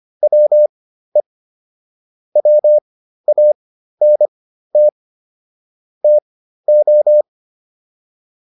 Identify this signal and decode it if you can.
Morse code, heard as a keyed tone, about 25 wpm, spaced as for 13 wpm, 600 Hz. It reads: WE WANT TO